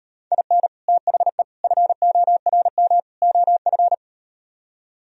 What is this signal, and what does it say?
Morse code, heard as a keyed tone, about 38 wpm, 700 Hz. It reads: IN THE FORM OF